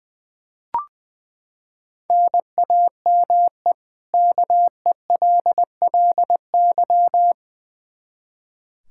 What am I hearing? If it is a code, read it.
Morse code, heard as a keyed tone, about 21 wpm, 700 Hz. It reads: NAME KELLY
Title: Practice QSO - Exchange 1 - Beginning - 20wpm